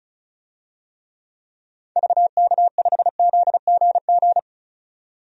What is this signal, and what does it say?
Morse code, heard as a keyed tone, about 36 wpm, 700 Hz. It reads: VK5ZGG
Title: DXCC-100 Call Signs - 6 Characters - 35wpm